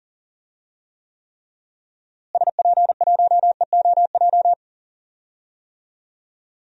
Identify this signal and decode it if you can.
Morse code, heard as a keyed tone, about 40 wpm, 700 Hz. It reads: SP1EOJ
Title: DXCC-100 Call Signs - 6 Characters - Repeated 3x - 40wpm